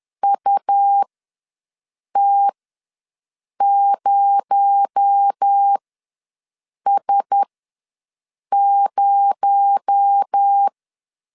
Morse code, heard as a keyed tone, about 10 wpm, spaced as for 6 wpm, 800 Hz. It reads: UT0S0
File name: LEZIONI / 6 PKX0 / 6 PKX0 ws 10wpm 4220